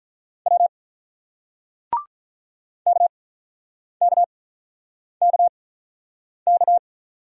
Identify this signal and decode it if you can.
Morse code, heard as a keyed tone, about 47 wpm, spaced as for 9 wpm, 700 Hz. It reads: W KKKK